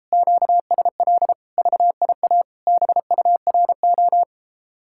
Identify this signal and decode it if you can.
Morse code, heard as a keyed tone, about 33 wpm, 700 Hz. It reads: QSL VIA BURO